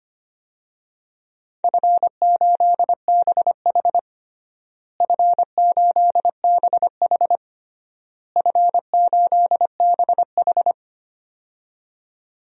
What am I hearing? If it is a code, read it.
Morse code, heard as a keyed tone, about 25 wpm, 700 Hz. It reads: F8BH F8BH F8BH
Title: DXCC-100 Call Signs - 4 Characters - Repeated 3x - 25wpm